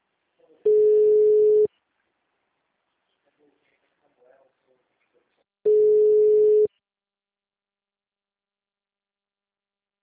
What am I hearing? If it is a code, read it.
Ringback tone (Europe, 425 Hz).